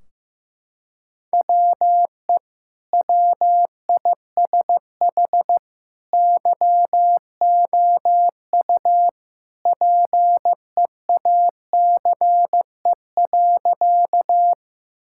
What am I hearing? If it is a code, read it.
Morse code, heard as a keyed tone, about 15 wpm, 700 Hz. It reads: WE WISH YOU PEACE.
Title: Sentences from Top 1600 Words 15wpm